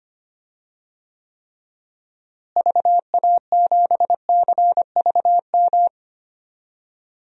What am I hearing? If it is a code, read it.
Morse code, heard as a keyed tone, about 25 wpm, 700 Hz. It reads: VA7CVM
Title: DXCC-100 Call Signs - 6 Characters - Repeated 2x - 25wpm